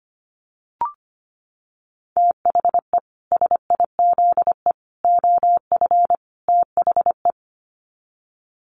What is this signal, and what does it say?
Morse code, heard as a keyed tone, about 25 wpm, 700 Hz. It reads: THE SIZE OF THE
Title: Sets of 4 Words 25wpm